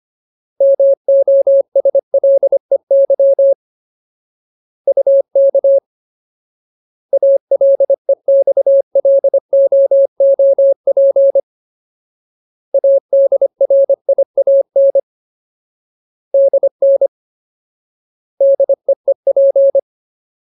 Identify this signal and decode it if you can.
Morse code, heard as a keyed tone, about 25 wpm, 550 Hz. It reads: MOSLEY UK ALEXLOOP ADRIAN DN DEEP
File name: Ultimate Word List / 25WPM Ultimate